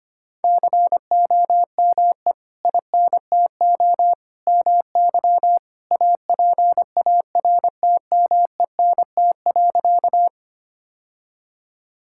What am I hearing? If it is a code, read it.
Morse code, heard as a keyed tone, about 25 wpm, 700 Hz. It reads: COME INTO MY APARTMENT.
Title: Sentences from Top 1000 Words - Review Part 6 - 25wpm